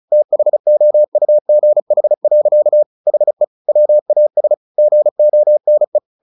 Morse code, heard as a keyed tone, about 34 wpm, 600 Hz. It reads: THOUGH. HE WAS GONE
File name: the old man and the sea / the old man and the sea 1 35wpm